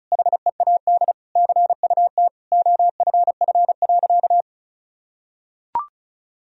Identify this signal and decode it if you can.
Morse code, heard as a keyed tone, about 34 wpm, 700 Hz. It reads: HEAD CUT OFF.
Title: Sentences from Top 1000 Words - Review Part 1 - 35wpm